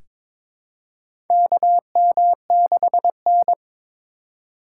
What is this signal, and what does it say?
Morse code, heard as a keyed tone, about 22 wpm, 700 Hz. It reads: KM6N
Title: USA Call Signs - 4 Characters - 2x Character Spacing - 22wpm